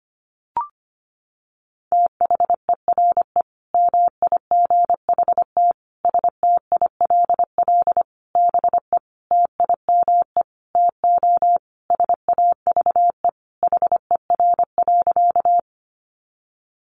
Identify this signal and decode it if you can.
Morse code, heard as a keyed tone, about 25 wpm, 700 Hz. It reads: THERE MIGHT STILL BE TIME TO SAVE HER.
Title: Sentences from Top 1100 Words 25wpm